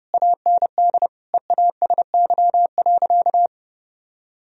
Morse code, heard as a keyed tone, about 30 wpm, 700 Hz. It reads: AND EASY.
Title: Sentences from Top 1000 Words - Review Part 3 - 30wpm